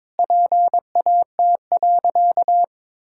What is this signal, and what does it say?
Morse code, heard as a keyed tone, about 22 wpm, 700 Hz. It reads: PAT.
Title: Pangram Challenge v2 - 22wpm